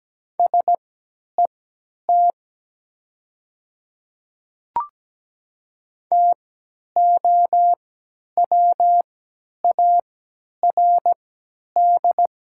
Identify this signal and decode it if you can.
Morse code, heard as a keyed tone, about 17 wpm, spaced as for 10 wpm, 700 Hz. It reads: SET TOWARD
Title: 3x Character Spacing - Top 500 Words - 17wpm